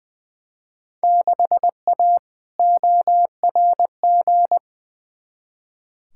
Morse code, heard as a keyed tone, about 20 wpm, 700 Hz. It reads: 6A ORG
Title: Field Day Exchanges Encore - 20wpm